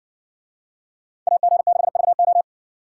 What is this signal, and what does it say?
Morse code, heard as a keyed tone, about 60 wpm, 700 Hz. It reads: AG6VO